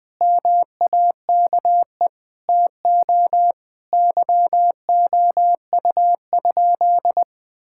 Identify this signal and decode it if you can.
Morse code, heard as a keyed tone, about 21 wpm, 700 Hz. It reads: MAKE TO YOU?
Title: Sentences from Top 1000 Words - Review Part 5 - 20wpm